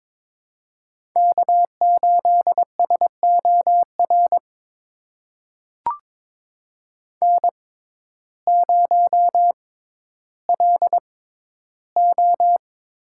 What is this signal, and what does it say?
Morse code, heard as a keyed tone, about 22 wpm, spaced as for 12 wpm, 700 Hz. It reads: K8SOR N0LO
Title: USA Call Signs - 5 Characters - 6x Character Spacing - 22wpm